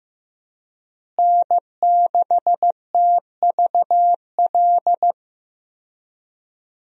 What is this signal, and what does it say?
Morse code, heard as a keyed tone, about 15 wpm, 700 Hz. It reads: N6TVL
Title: USA Call Signs - 5 Characters - Repeated 2x - 15wpm